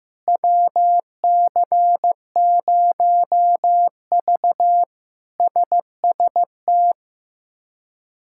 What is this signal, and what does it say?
Morse code, heard as a keyed tone, about 15 wpm, 700 Hz. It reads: WC0V SST
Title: K1USN SST QSO Practice - 15wpm